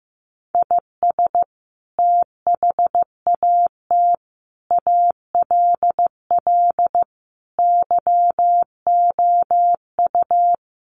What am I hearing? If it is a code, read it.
Morse code, heard as a keyed tone, about 14 wpm, 700 Hz. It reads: IS THAT ALL YOU
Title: Sentences from Top 200 Words 15wpm